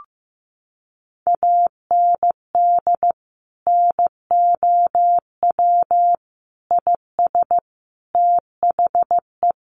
Morse code, heard as a keyed tone, about 15 wpm, 700 Hz. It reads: AND NOW IS THE
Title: Sentences from Top 800 Words 15wpm